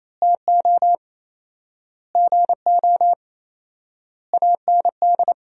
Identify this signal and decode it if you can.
Morse code, heard as a keyed tone, about 28 wpm, 700 Hz. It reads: TO GO AND
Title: 4x Word-Spacing - Sentences from Top 200 Words - 28wpm